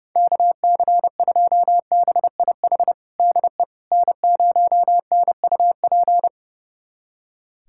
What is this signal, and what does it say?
Morse code, heard as a keyed tone, about 30 wpm, 700 Hz. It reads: KC2BIH DE N0NUP
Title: USA Callsign DE Callsign 30wpm